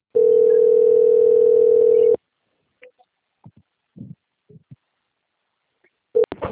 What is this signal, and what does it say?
Ringback tone (North America), 440 + 480 Hz.
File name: out- - - - - .48262